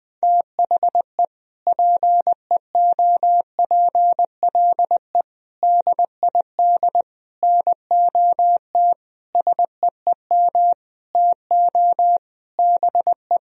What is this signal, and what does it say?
Morse code, heard as a keyed tone, about 20 wpm, 700 Hz. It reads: THE PEOPLE DID NOT SEEM TO BE
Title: Sentences from Top 1000 Words - Review Part 1 - 20wpm